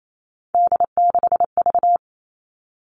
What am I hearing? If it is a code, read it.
Morse code, heard as a keyed tone, about 28 wpm, 700 Hz. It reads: D6V
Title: Three Challenging Letter-Number Combinations 28wpm